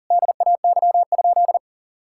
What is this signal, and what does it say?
Morse code, heard as a keyed tone, about 40 wpm, 700 Hz. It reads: DAY?
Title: Sentences from Top 1000 Words - Review Part 6 - 40wpm